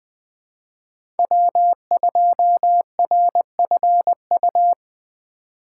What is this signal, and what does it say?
Morse code, heard as a keyed tone, about 20 wpm, 700 Hz. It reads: W2RFU
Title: USA Call Signs - 5 Characters - 3x Character Spacing - 20wpm